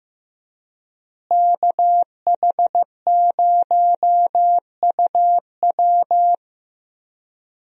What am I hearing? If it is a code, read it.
Morse code, heard as a keyed tone, about 15 wpm, 700 Hz. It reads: KH0UW
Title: USA Call Signs - 5 Characters - Repeated 3x - 15wpm